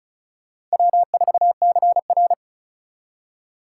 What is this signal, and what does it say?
Morse code, heard as a keyed tone, about 34 wpm, 700 Hz. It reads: W4CR
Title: USA Call Signs - 4 Characters - 2x Character Spacing - 35wpm